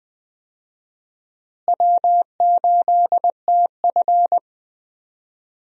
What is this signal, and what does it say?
Morse code, heard as a keyed tone, about 20 wpm, 700 Hz. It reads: W8TF
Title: USA Call Signs - 4 Characters - Repeated 3x - 20wpm